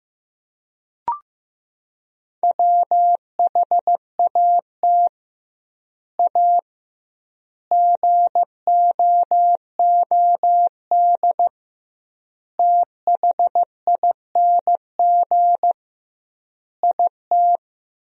Morse code, heard as a keyed tone, about 15 wpm, 700 Hz. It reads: WHAT A GOOD THING IT
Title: 2x Word-Spacing - Sentences from Top 200 Words - 15wpm